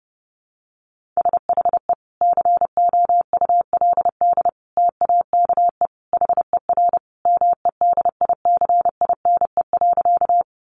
Morse code, heard as a keyed tone, about 28 wpm, 700 Hz. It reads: SHE COULD TAKE HER MEDICINE.
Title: Sentences from Top 900 Words 30wpm